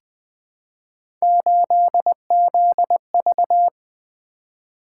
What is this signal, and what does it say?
Morse code, heard as a keyed tone, about 20 wpm, 700 Hz. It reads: 8ZV
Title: Three Letter-Number Combinations - 20wpm